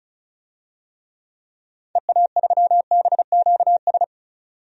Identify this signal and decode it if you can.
Morse code, heard as a keyed tone, about 36 wpm, 700 Hz. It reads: EA3BQS
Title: DXCC-100 Call Signs - 6 Characters - 35wpm